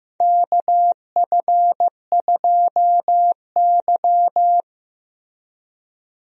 Morse code, heard as a keyed tone, about 15 wpm, 700 Hz. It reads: KF2Y
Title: USA Call Signs - 4 Characters - Repeated 2x - 15wpm